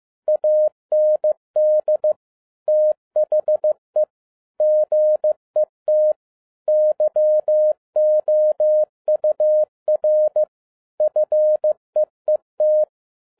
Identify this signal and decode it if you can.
Morse code, heard as a keyed tone, about 15 wpm, 610 Hz. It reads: AND THE GET YOUR FEET
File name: 2011-10-mp3 / n12-JL3YMV 15wpm